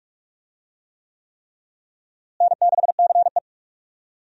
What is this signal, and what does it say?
Morse code, heard as a keyed tone, about 47 wpm, 700 Hz. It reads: N6CE
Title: USA Call Signs - 4 Characters - Repeated 3x - 45wpm